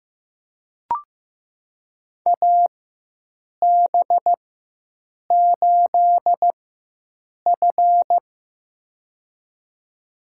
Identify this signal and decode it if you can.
Morse code, heard as a keyed tone, about 15 wpm, spaced as for 7 wpm, 700 Hz. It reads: AB8F
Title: USA Call Signs - 4x Character Spacing - 15wpm